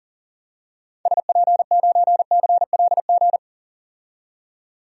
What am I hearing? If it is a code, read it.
Morse code, heard as a keyed tone, about 37 wpm, 700 Hz. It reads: SP9CLG
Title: DXCC-100 Call Signs - 6 Characters - 40wpm